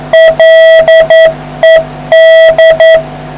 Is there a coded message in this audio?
Yes: Morse code from 0.1 to 3.0 s, heard as a keyed tone, about 10 wpm, 650 Hz, with noise. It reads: LED